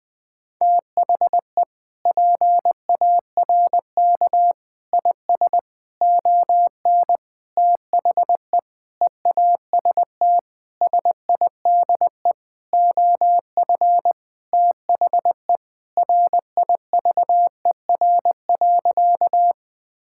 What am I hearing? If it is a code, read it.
Morse code, heard as a keyed tone, about 21 wpm, 700 Hz. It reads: THE PARK IS ON THE EAST SIDE OF THE RIVER.
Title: Sentences with 6-Word Phrases - 20wpm